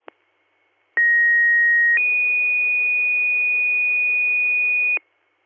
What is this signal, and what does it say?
Two-tone page, heard as a one-second tone, then a three-second tone, about 1900 then 2450 Hz, with noise. It is heard from 1.0 to 5.0 s.